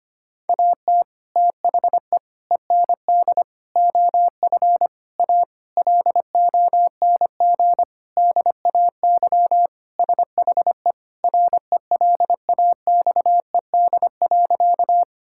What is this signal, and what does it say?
Morse code, heard as a keyed tone, about 25 wpm, 700 Hz. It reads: AT THE END OF A LONG DAY SHE RELAXED.
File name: practice / 25wpm / Sentences with 4-Word Phrases - 25wpm